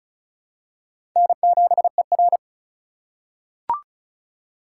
Morse code, heard as a keyed tone, about 34 wpm, 700 Hz. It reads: N7ER
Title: USA Call Signs - 4 Characters - 2x Character Spacing - 35wpm